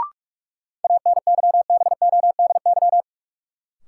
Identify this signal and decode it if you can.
Morse code, heard as a keyed tone, about 43 wpm, 700 Hz. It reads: ANYBODY
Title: Sets of 5 Words - New Word Drill - 45wpm